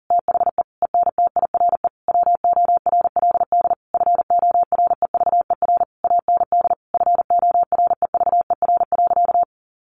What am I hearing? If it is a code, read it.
Morse code, heard as a keyed tone, about 40 wpm, 700 Hz. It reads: THE ENTIRE WORLD FOREVER AND FOREVER.
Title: A Sound Of Thunder 40wpm